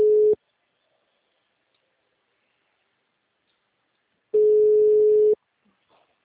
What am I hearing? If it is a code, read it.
Ringback tone (Europe, 425 Hz).
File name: out- -2035- - - .103